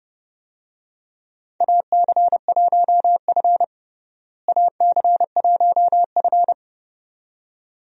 Morse code, heard as a keyed tone, about 32 wpm, 700 Hz. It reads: AC1F AC1F
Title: USA Call Signs - 4 Characters - Repeated 2x - 30wpm